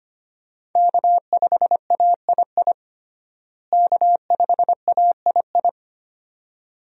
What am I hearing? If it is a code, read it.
Morse code, heard as a keyed tone, about 25 wpm, 700 Hz. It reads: K5AII K5AII
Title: USA Call Signs - 5 Characters - Repeated 2x - 25wpm